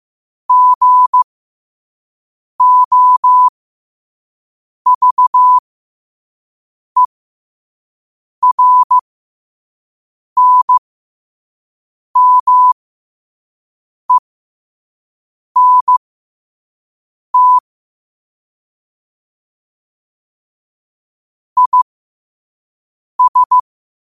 Morse code, heard as a keyed tone, about 15 wpm, spaced as for 5 wpm, 1000 Hz. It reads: GOVERNMENT IS
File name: qotdfiles / 5wpm